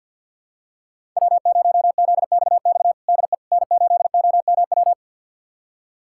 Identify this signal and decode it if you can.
Morse code, heard as a keyed tone, about 53 wpm, 700 Hz. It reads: W0ZXX DE N8OMW
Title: USA Callsign DE Callsign - 5 Characters - 50wpm